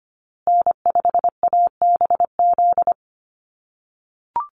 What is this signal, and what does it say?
Morse code, heard as a keyed tone, about 25 wpm, 700 Hz. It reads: N5ABZ
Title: USA Call Signs Repeated 2x 25wpm